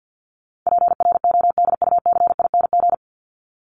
Morse code, heard as a keyed tone, about 50 wpm, 700 Hz. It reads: PRODUCING
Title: ING Suffix - Top 100 Words with Base Words 50wpm